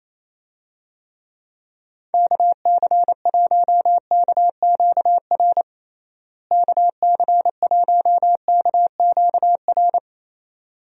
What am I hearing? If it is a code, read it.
Morse code, heard as a keyed tone, about 28 wpm, 700 Hz. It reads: KC1KQR KC1KQR
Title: USA Call Signs - 6 Characters - Repeated 2x - 28wpm